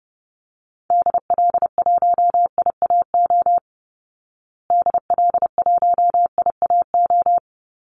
Morse code, heard as a keyed tone, about 28 wpm, 700 Hz. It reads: DL1IAO DL1IAO
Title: Call Signs Repeated 2x - 30wpm